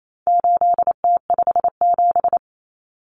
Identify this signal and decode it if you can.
Morse code, heard as a keyed tone, about 28 wpm, 700 Hz. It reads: 8T57
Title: Common Cut Numbers - 4-Digit 28wpm